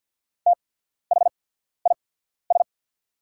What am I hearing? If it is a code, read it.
Morse code, heard as a keyed tone, about 50 wpm, spaced as for 14 wpm, 700 Hz. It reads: THIS
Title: 8x Character Spacing - Top 100 Words - 50wpm